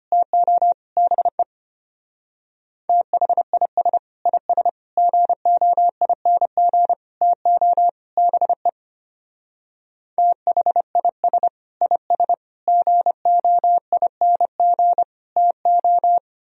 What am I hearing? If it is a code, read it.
Morse code, heard as a keyed tone, about 31 wpm, 700 Hz. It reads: TO BE THIS IS GOING TO BE THIS IS GOING TO
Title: Speed-Racing - Sets of 5 Words 25wpm